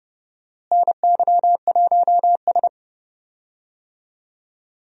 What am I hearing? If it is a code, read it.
Morse code, heard as a keyed tone, about 30 wpm, 700 Hz. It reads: NY1S
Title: USA Call Signs - 4 Characters - Repeated 3x - 30wpm